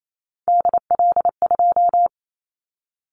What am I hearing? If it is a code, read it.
Morse code, heard as a keyed tone, about 28 wpm, 700 Hz. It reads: DL2
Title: Callsign Prefixes for Top 25 DXCC - 28wpm